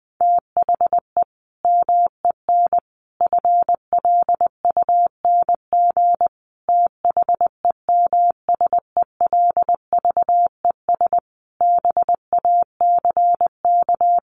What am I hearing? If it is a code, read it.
Morse code, heard as a keyed tone, about 20 wpm, 700 Hz. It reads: THE MEN FLUNG THEMSELVES BACK
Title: A Sound Of Thunder 20wpm